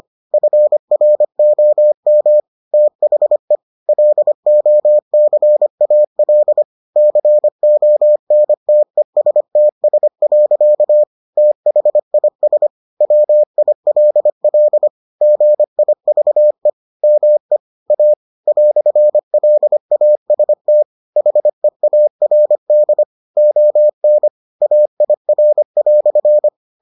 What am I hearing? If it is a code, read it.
Morse code, heard as a keyed tone, about 25 wpm, 600 Hz. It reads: FROM THE LOCAL CONTESTS. THIS WILL GIVE ME A "LAST HEARD ON AIR"